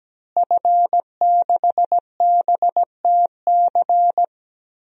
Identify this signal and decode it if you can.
Morse code, heard as a keyed tone, about 17 wpm, 700 Hz. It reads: F6BTC
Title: DXCC-100 Call Signs - 5 Characters - Repeated 2x - 17wpm